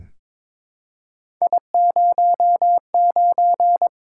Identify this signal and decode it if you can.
Morse code, heard as a keyed tone, about 22 wpm, 700 Hz. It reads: I09
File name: practice / 22wpm / Three Letter-Number Combinations - 22wpm